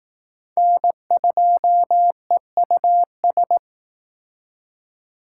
Morse code, heard as a keyed tone, about 20 wpm, 700 Hz. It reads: N2EUS